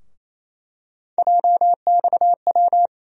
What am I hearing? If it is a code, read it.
Morse code, heard as a keyed tone, about 27 wpm, 700 Hz. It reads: JXW
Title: USA Call Sign Postfixes - 3 Letters 28wpm